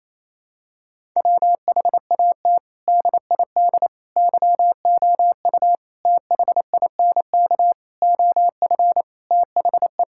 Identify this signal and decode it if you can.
Morse code, heard as a keyed tone, about 27 wpm, 700 Hz. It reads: WHAT DID YOU THINK OF THE
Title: Sentences from Top 1000 Words - Review Part 6 - 28wpm